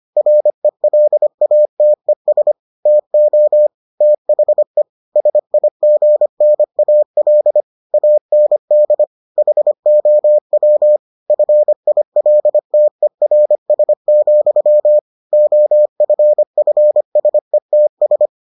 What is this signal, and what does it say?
Morse code, heard as a keyed tone, about 25 wpm, 600 Hz. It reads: RELATES TO THE SIGNAL AND HOW FILTERS, OFFSETS